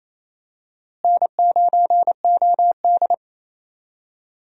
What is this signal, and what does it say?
Morse code, heard as a keyed tone, about 27 wpm, 700 Hz. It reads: N9OD
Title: USA Call Signs - 4 Characters - Repeated 2x - 28wpm